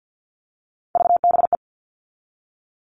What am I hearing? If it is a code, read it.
Morse code, heard as a keyed tone, about 50 wpm, 700 Hz. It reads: VBE
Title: All Cut Numbers - 3-Digit 50wpm